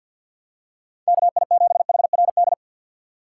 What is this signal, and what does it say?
Morse code, heard as a keyed tone, about 54 wpm, 700 Hz. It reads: K I 7 H R D
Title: USA Call Signs - 6 Characters - 8x Character Spacing - 50wpm